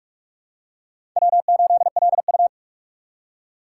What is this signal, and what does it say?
Morse code, heard as a keyed tone, about 44 wpm, 690 Hz. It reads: W8LU
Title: USA Call Signs - 4 Characters - 3x Character Spacing - 45wpm